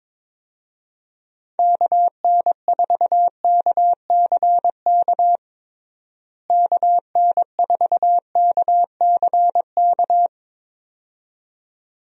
Morse code, heard as a keyed tone, about 22 wpm, 700 Hz. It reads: KN4KCK KN4KCK
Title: USA Call Signs - 6 Characters - Repeated 2x - 22wpm